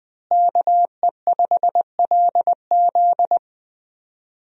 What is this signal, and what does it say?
Morse code, heard as a keyed tone, about 20 wpm, 700 Hz. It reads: KE5LZ
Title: USA Call Signs - 5 Characters - Repeated 3x - 20wpm